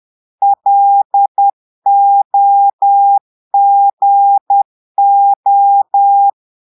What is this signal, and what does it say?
Morse code, heard as a keyed tone, about 10 wpm, 800 Hz. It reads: LOGO
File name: portugais texte en clair 2 vitesse 10 mots minutes